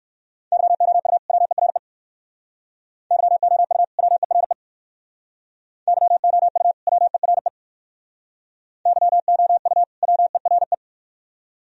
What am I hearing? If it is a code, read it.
Morse code, heard as a keyed tone, about 59 wpm, 700 Hz. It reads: YOU WERE YOU WERE YOU WERE YOU WERE